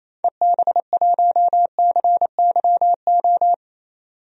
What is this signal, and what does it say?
Morse code, heard as a keyed tone, about 28 wpm, 700 Hz. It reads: EB1CYO
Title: DXCC-100 Call Signs - 6 Characters - 28wpm